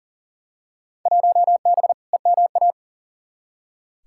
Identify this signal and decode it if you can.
Morse code, heard as a keyed tone, about 40 wpm, 700 Hz. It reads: 1B EMA
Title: Field Day Exchanges - 40wpm